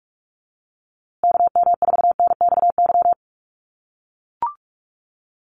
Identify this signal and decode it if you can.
Morse code, heard as a keyed tone, about 44 wpm, 700 Hz. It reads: KM4NXY
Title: USA Call Signs Repeated 2x 45wpm